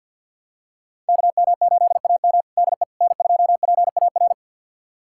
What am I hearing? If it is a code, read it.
Morse code, heard as a keyed tone, about 52 wpm, 700 Hz. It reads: KM8AM DE N2PAR
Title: USA Callsign DE Callsign - 5 Characters - 50wpm